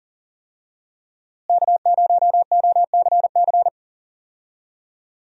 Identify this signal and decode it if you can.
Morse code, heard as a keyed tone, about 40 wpm, 700 Hz. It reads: K0OCC
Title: USA Call Signs - 5 Characters - Repeated 2x - 40wpm